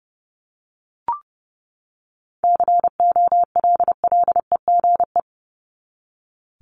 Morse code, heard as a keyed tone, about 30 wpm, 700 Hz. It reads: COLLEGE
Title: Top 1001-1250 Words 30wpm